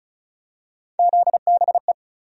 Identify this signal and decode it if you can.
Morse code, heard as a keyed tone, about 34 wpm, 700 Hz. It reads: ZBE
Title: USA Call Sign Postfixes - 3 Letters 35wpm